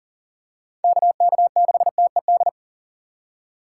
Morse code, heard as a keyed tone, about 40 wpm, 700 Hz. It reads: KK6TED
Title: USA Call Signs - 6 Characters - 2x Character Spacing - 40wpm